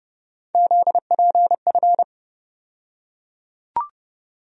Morse code, heard as a keyed tone, about 30 wpm, 700 Hz. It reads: ZPF
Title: USA Call Sign Postfixes - 3 Letters 30wpm